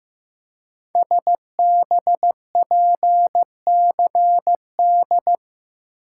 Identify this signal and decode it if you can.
Morse code, heard as a keyed tone, about 15 wpm, 700 Hz. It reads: SBPCD
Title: Code Groups - 5 Letters - 15wpm